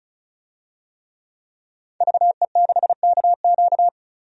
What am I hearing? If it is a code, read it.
Morse code, heard as a keyed tone, about 36 wpm, 700 Hz. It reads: VE6KQ
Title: DXCC-100 Call Signs - 5 Characters - 35wpm